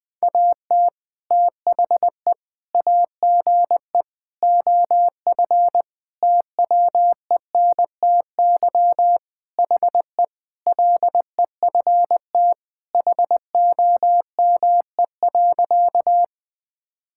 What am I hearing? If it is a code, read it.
Morse code, heard as a keyed tone, about 21 wpm, 700 Hz. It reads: AT THE AGE OF TWENTY HE LEFT HOME.
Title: Sentences with 4-Word Phrases - 20wpm